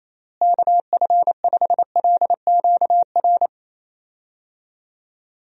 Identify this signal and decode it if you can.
Morse code, heard as a keyed tone, about 27 wpm, 700 Hz. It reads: KF5LQR
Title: USA Call Signs - 6 Characters - Repeated 3x - 28wpm